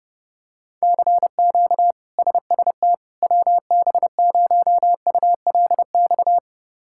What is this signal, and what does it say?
Morse code, heard as a keyed tone, about 32 wpm, 700 Hz. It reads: CQ SST WB0ULX
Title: K1USN SST QSO Practice - 30wpm